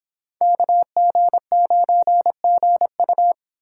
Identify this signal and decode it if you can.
Morse code, heard as a keyed tone, about 27 wpm, 700 Hz. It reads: KG9GU
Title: Speed Racing - USA 2x2 Call Signs 22wpm